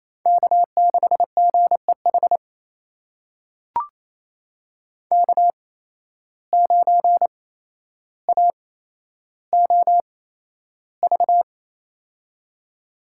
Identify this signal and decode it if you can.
Morse code, heard as a keyed tone, about 28 wpm, spaced as for 12 wpm, 700 Hz. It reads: K6GEH K9AOV
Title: USA Call Signs - 5 Characters - 8x Character Spacing - 28wpm